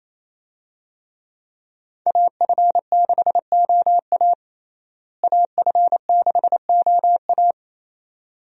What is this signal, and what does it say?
Morse code, heard as a keyed tone, about 28 wpm, 700 Hz. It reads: AF6OA AF6OA
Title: USA Call Signs - 5 Characters - Repeated 2x - 28wpm